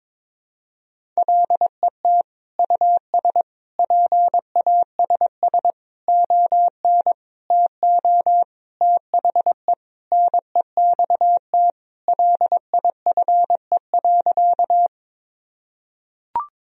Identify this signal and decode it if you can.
Morse code, heard as a keyed tone, about 21 wpm, 700 Hz. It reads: LET US PASS ON TO THE NEXT LIFE.
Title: Sentences from Top 1000 Words - Review Part 1 - 22wpm